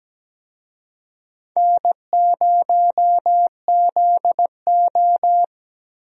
Morse code, heard as a keyed tone, about 17 wpm, 700 Hz. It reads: N0ZO
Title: USA Call Signs - 4 Characters - Repeated 3x - 17wpm